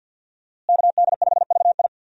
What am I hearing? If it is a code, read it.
Morse code, heard as a keyed tone, about 52 wpm, 700 Hz. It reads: KD5VI
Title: USA Call Signs - 5 Characters - Repeated 3x - 50wpm